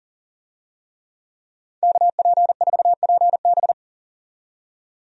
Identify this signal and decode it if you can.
Morse code, heard as a keyed tone, about 40 wpm, 700 Hz. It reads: KP4PB
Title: DXCC-100 Call Signs - 5 Characters - 40wpm